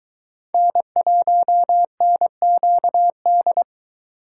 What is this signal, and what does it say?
Morse code, heard as a keyed tone, about 24 wpm, 700 Hz. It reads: N1NQD